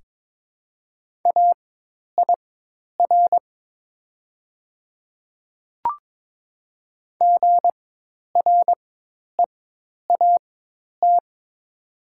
Morse code, heard as a keyed tone, about 22 wpm, spaced as for 10 wpm, 700 Hz. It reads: AIR GREAT
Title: 4x Character Spacing - Top 300 Words - 22wpm